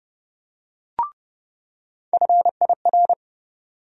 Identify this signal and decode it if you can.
Morse code, heard as a keyed tone, about 32 wpm, 700 Hz. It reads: FIR
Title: USA Call Sign Postfixes - 3 Letters 30wpm